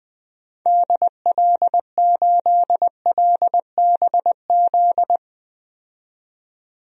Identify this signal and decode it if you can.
Morse code, heard as a keyed tone, about 20 wpm, 700 Hz. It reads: DL8LBZ
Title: DXCC-100 Call Signs - 6 Characters - 20wpm